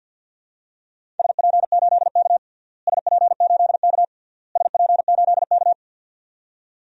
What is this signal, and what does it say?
Morse code, heard as a keyed tone, about 48 wpm, 700 Hz. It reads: SP8K SP8K SP8K
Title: DXCC-100 Call Signs - 4 Characters - Repeated 3x - 50wpm